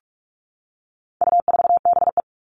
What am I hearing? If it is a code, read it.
Morse code, heard as a keyed tone, about 44 wpm, 700 Hz. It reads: U4BE
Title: All Cut Numbers - 4-Digit 45wpm